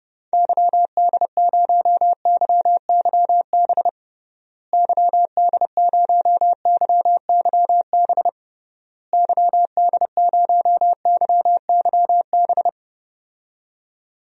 Morse code, heard as a keyed tone, about 30 wpm, 700 Hz. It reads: YD0YYB YD0YYB YD0YYB
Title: DXCC-100 Call Signs - 6 Characters - Repeated 3x - 30wpm